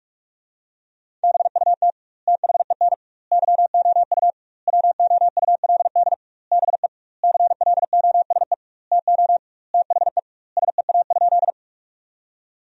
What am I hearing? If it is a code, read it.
Morse code, heard as a keyed tone, about 41 wpm, 700 Hz. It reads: BUT THEN YOU WOULD BE CLOSE TO THE SEA?
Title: 2x Word-Spacing - Sentences from Top 300 Words - 45wpm